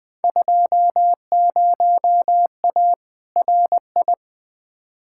Morse code, heard as a keyed tone, about 20 wpm, 700 Hz. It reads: 20A RI
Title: Field Day Exchanges Encore - 20wpm